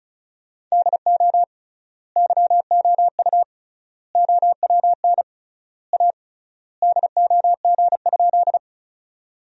Morse code, heard as a keyed tone, about 36 wpm, 700 Hz. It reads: DO YOU OWN A DOG?
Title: 3x Word-Spacing - Sentences from Top 400 Words - 35wpm